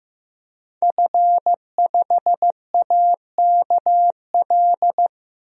Morse code, heard as a keyed tone, about 15 wpm, 700 Hz. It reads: F5AKL
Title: DXCC-100 Call Signs - 5 Characters - Repeated 2x - 15wpm